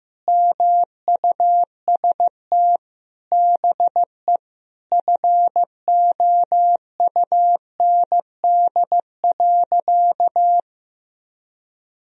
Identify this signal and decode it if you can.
Morse code, heard as a keyed tone, about 16 wpm, 700 Hz. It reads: MUST BE FOUND.
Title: Sentences from Top 1000 Words - Review Part 4 - 15wpm